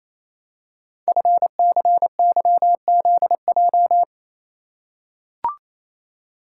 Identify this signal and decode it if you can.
Morse code, heard as a keyed tone, about 27 wpm, 700 Hz. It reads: FCYZJ
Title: Code Groups - 5 Letters - 28wpm